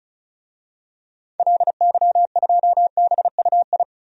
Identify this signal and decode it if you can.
Morse code, heard as a keyed tone, about 34 wpm, 700 Hz. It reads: LY2BUI
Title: DXCC-100 Call Signs - 6 Characters - 35wpm